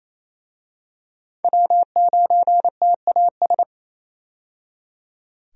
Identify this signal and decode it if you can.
Morse code, heard as a keyed tone, about 28 wpm, 700 Hz. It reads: W9TAS
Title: USA Call Signs - 5 Characters - 28wpm